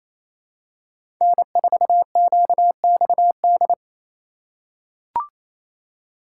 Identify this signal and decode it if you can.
Morse code, heard as a keyed tone, about 28 wpm, 700 Hz. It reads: N4QXD
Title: USA Call Signs - 5 Characters - 4x Character Spacing - 28wpm